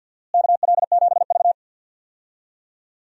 Morse code, heard as a keyed tone, about 49 wpm, 700 Hz. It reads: KL7V
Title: DXCC-100 Call Signs - 4 Characters - Repeated 2x - 50wpm